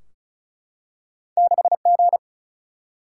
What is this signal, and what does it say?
Morse code, heard as a keyed tone, about 35 wpm, 700 Hz. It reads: 6G
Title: Single-Letter Single-Number 35wpm